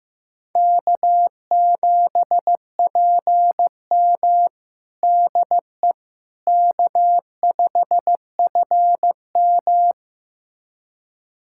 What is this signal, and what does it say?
Morse code, heard as a keyed tone, about 16 wpm, 700 Hz. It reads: K7PM DE K5FM
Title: USA Callsign DE Callsign - 4 Characters - 15wpm